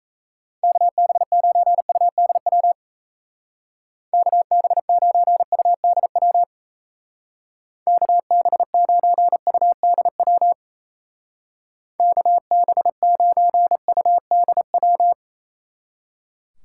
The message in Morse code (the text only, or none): KB9UDW KB9UDW KB9UDW KB9UDW